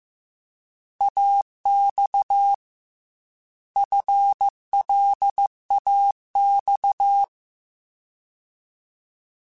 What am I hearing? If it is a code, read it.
Morse code, heard as a keyed tone, about 15 wpm, 800 Hz. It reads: AX FLAX
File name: FL / FL mp3s / Fast FL @13